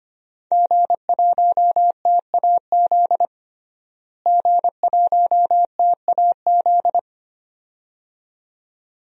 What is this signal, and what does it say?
Morse code, heard as a keyed tone, about 25 wpm, 700 Hz. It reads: G1TAZ G1TAZ
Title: DXCC-100 Call Signs - 5 Characters - Repeated 2x - 25wpm